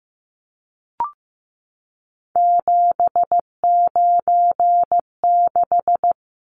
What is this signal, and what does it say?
Morse code, heard as a keyed tone, about 15 wpm, 700 Hz. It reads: 796
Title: Numbers - 3-Digit 15wpm